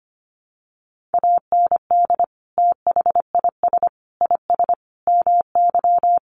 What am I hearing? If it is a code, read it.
Morse code, heard as a keyed tone, about 25 wpm, 700 Hz. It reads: AND THIS IS MY
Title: Sentences from Top 1200 Words 25wpm